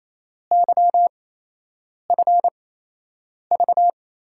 Morse code, heard as a keyed tone, about 28 wpm, spaced as for 8 wpm, 700 Hz. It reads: YFV
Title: USA Call Signs - 6 Characters - 8x Character Spacing - 28wpm